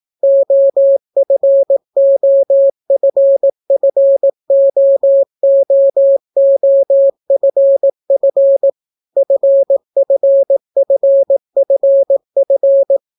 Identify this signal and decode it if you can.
Morse code, heard as a keyed tone, about 18 wpm, 550 Hz. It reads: OFOFFOOOFF FFFFF